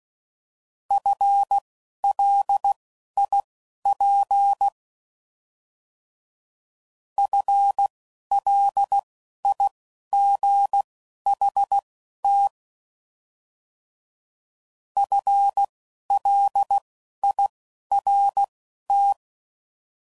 Morse code, heard as a keyed tone, about 16 wpm, spaced as for 11 wpm, 800 Hz. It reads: FLIP FLIGHT FLIRT